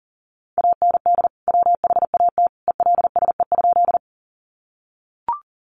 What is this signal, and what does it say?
Morse code, heard as a keyed tone, about 40 wpm, 700 Hz. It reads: AND WHAT ELSE?
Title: Sentences from Top 800 Words 40wpm